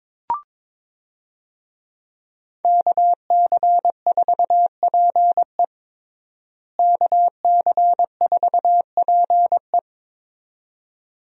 Morse code, heard as a keyed tone, about 22 wpm, 700 Hz. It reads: KC4PE KC4PE
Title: USA Call Signs - 5 Characters - Repeated 2x - 22wpm